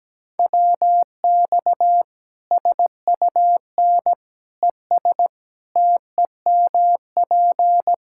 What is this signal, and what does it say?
Morse code, heard as a keyed tone, about 17 wpm, 700 Hz. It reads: WX SUN ES TEMP
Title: Practice QSO - Exchange 2 - Info - 17wpm